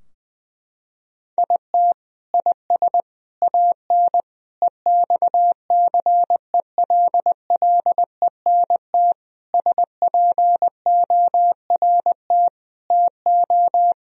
Morse code, heard as a keyed tone, about 20 wpm, 700 Hz. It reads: IT IS AN EXCELLENT SPORT TO
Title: Sentences from Top 1000 Words - Review Part 4 - 20wpm